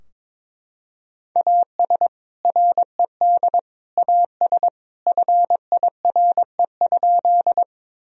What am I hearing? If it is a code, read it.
Morse code, heard as a keyed tone, about 22 wpm, 700 Hz. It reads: AS RED AS FIRE?
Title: Sentences from Top 1000 Words - Review Part 2 - 22wpm